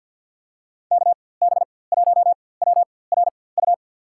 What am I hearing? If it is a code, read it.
Morse code, heard as a keyed tone, about 51 wpm, spaced as for 23 wpm, 700 Hz. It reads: KB1WRU
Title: USA Call Signs - 6 Characters - 4x Character Spacing - 50wpm